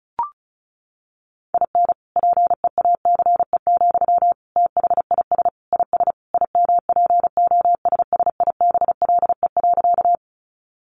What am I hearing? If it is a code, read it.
Morse code, heard as a keyed tone, about 34 wpm, 700 Hz. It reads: IN PEACE, THIS IS IMPOSSIBLE.
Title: Sentences from Top 1500 Words 35wpm